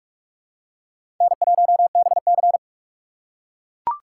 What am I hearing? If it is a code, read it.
Morse code, heard as a keyed tone, about 42 wpm, 700 Hz. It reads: N1BC
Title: USA Call Signs - 6x Character Spacing - 45wpm